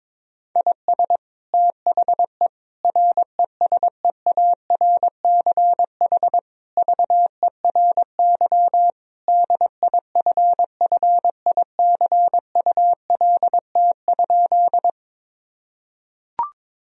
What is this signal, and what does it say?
Morse code, heard as a keyed tone, about 22 wpm, 700 Hz. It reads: IS THE RESEARCH VERY DIFFICULT?
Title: Sentences from Top 1000 Words - Review Part 4 - 22wpm